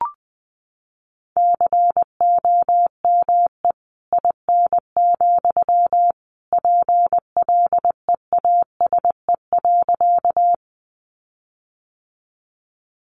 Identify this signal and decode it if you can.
Morse code, heard as a keyed tone, about 20 wpm, 700 Hz. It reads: COME IN, PLEASE.
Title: Sentences from Top 1500 Words 20wpm